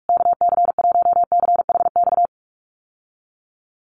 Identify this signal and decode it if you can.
Morse code, heard as a keyed tone, about 44 wpm, 700 Hz. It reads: KC1CHX
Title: USA Call Signs Repeated 3x 45wpm